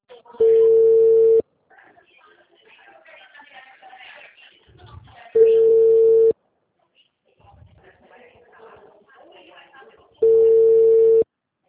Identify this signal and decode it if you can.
Ringback tone (China, 450 Hz), with speech.
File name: out- -2020- - - .69179